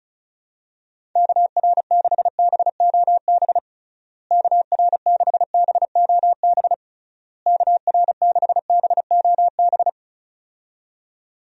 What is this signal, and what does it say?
Morse code, heard as a keyed tone, about 36 wpm, 700 Hz. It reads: KR6BOB KR6BOB KR6BOB
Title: USA Call Signs - 6 Characters - Repeated 3x - 35wpm